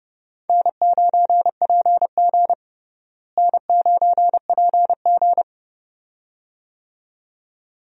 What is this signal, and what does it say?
Morse code, heard as a keyed tone, about 31 wpm, 700 Hz. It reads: N9PG N9PG
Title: USA Call Signs - 4 Characters - Repeated 2x - 30wpm